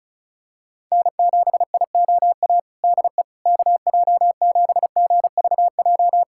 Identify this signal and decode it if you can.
Morse code, heard as a keyed tone, about 36 wpm, 690 Hz. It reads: N7IOA DE KJ7GVJ
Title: USA Callsign DE Callsign 35wpm